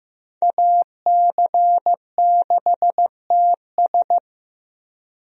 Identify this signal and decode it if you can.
Morse code, heard as a keyed tone, about 15 wpm, 700 Hz. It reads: AC6TS
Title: USA Call Signs - 5 Characters - Repeated 2x - 15wpm